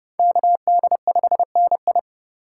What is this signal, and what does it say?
Morse code, heard as a keyed tone, about 27 wpm, 700 Hz. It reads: KD5NI